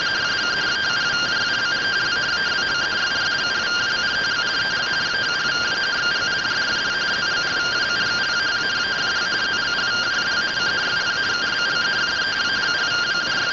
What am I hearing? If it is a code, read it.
Two-tone teleprinter signal, about 170 Hz shift.